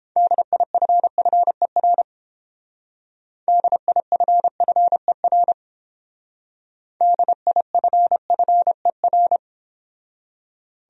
Morse code, heard as a keyed tone, about 30 wpm, 700 Hz. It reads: DIFFER DIFFER DIFFER